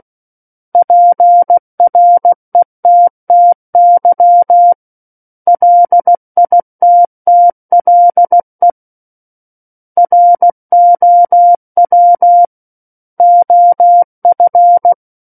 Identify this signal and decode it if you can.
Morse code, heard as a keyed tone, about 16 wpm, 700 Hz. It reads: PRETTY LITTLE ROW OF
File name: Book One / 07-Chapter 7 - 16 WPM